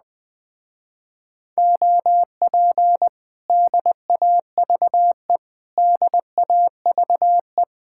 Morse code, heard as a keyed tone, about 20 wpm, 700 Hz. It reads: OP DAVE DAVE
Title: Speed-Racing Common QSO Phrases - 17wpm